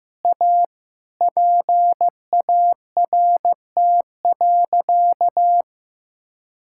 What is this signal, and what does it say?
Morse code, heard as a keyed tone, about 16 wpm, 700 Hz. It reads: A PART.
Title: Sentences from Top 1000 Words - Review Part 5 - 15wpm